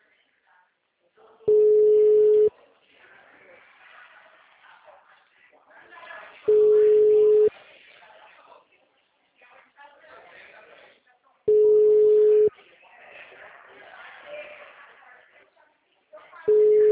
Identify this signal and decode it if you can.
Ringback tone (Europe, 425 Hz), with speech.